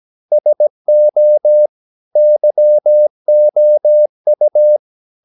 Morse code, heard as a keyed tone, about 17 wpm, 600 Hz. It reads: SO YOU
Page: Romanen Never af Ken Follet på engelsk.